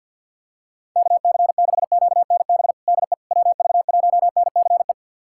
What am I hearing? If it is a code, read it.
Morse code, heard as a keyed tone, about 46 wpm, 700 Hz. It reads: KC6QNB DE WV1NCE
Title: USA Callsign DE Callsign - 6 Characters - 50wpm